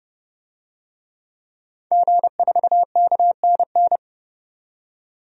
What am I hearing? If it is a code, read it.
Morse code, heard as a keyed tone, about 30 wpm, 700 Hz. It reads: G4KNN
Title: DXCC-100 Call Signs - 5 Characters - 30wpm